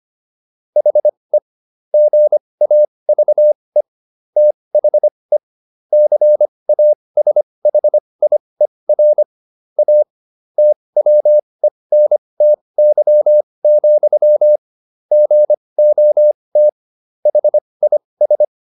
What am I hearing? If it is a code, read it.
Morse code, heard as a keyed tone, about 25 wpm, spaced as for 20 wpm, 600 Hz. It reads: HE GAVE THE CASHIER A TWENTY, GOT HIS